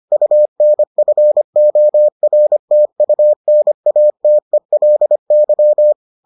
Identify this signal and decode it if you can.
Morse code, heard as a keyed tone, about 25 wpm, 600 Hz. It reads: UNFORTUNATELY